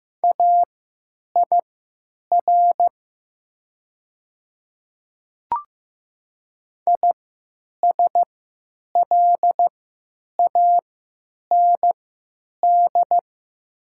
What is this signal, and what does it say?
Morse code, heard as a keyed tone, about 15 wpm, spaced as for 9 wpm, 700 Hz. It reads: AIR ISLAND
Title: 3x Character Spacing - Top 500 Words - 15wpm